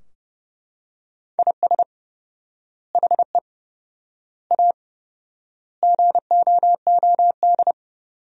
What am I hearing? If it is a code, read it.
Morse code, heard as a keyed tone, about 30 wpm, 700 Hz. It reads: IS HE A GOOD
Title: 4x Word-Spacing - Sentences from Top 200 Words - 30wpm